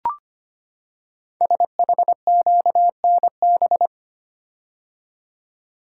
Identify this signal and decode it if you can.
Morse code, heard as a keyed tone, about 25 wpm, 700 Hz. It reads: SHQNB
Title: Code Groups - 5 Letters - 25wpm